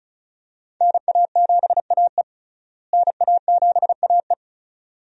Morse code, heard as a keyed tone, about 36 wpm, 700 Hz. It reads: NA7AE NA7AE
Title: USA Call Signs - 5 Characters - Repeated 2x - 35wpm